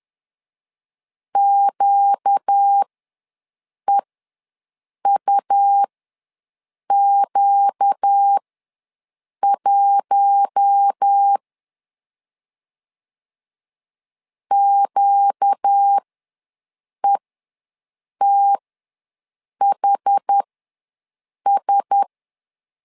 Morse code, heard as a keyed tone, about 10 wpm, spaced as for 6 wpm, 800 Hz. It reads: QEUQ1 QETHS
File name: LEZIONI / 4 Q1S ws 10wpm 2208